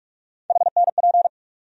Morse code, heard as a keyed tone, about 45 wpm, 700 Hz. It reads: HNP
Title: USA Call Sign Postfixes - 3 Letters 45wpm